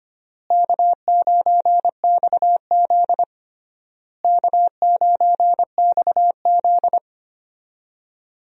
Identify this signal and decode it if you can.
Morse code, heard as a keyed tone, about 25 wpm, 700 Hz. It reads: K9XZ K9XZ
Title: USA Call Signs - 4 Characters - Repeated 3x - 25wpm